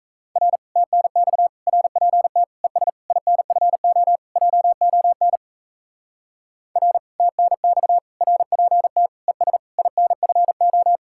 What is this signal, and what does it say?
Morse code, heard as a keyed tone, about 42 wpm, 700 Hz. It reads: R TNX RPT ES INFO JON R TNX RPT ES INFO